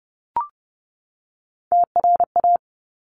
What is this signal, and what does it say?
Morse code, heard as a keyed tone, about 30 wpm, 700 Hz. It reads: TRA
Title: Top Three Letter Combinations in Words 30wpm